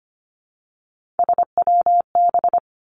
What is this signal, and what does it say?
Morse code, heard as a keyed tone, about 25 wpm, 700 Hz. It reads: SWB
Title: Three Challenging Letter-Number Combinations 25wpm